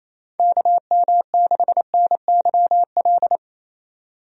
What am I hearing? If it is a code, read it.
Morse code, heard as a keyed tone, about 27 wpm, 700 Hz. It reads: KM6NYL
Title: USA Call Signs - 6 Characters - 28wpm